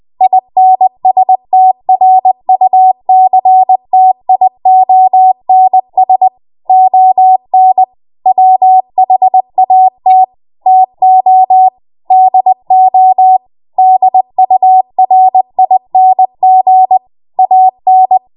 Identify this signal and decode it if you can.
Morse code, heard as a keyed tone, about 20 wpm, 750 Hz. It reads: INSTRUCTIONS ON WHAT TO DO DURING AN
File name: Morse / 20WPM